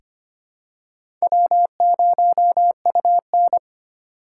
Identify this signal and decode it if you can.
Morse code, heard as a keyed tone, about 25 wpm, 700 Hz. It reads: W0UN
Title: USA Call Signs - 4 Characters - 8x Character Spacing - 25wpm